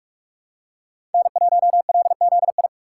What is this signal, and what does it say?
Morse code, heard as a keyed tone, about 42 wpm, 700 Hz. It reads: N1LZI
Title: USA Call Signs - 5 Characters - 3x Character Spacing - 45wpm